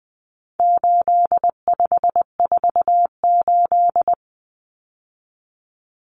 Morse code, heard as a keyed tone, about 20 wpm, 700 Hz. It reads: 8548
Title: Numbers - 4-Digit 20wpm